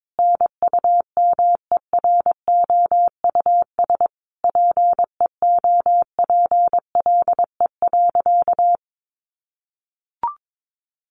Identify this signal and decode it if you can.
Morse code, heard as a keyed tone, about 22 wpm, 700 Hz. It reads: NUMEROUS PEOPLE.
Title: Sentences from Top 1400 Words 22wpm